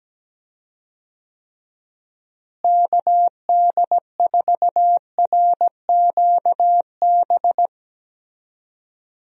Morse code, heard as a keyed tone, about 17 wpm, 700 Hz. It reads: KD4RQB
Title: USA Call Signs - 6 Characters - Repeated 3x - 17wpm